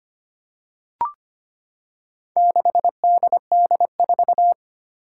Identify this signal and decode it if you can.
Morse code, heard as a keyed tone, about 25 wpm, 700 Hz. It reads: 6DD4
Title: Four Challenging Letter-Number Combinations 25wpm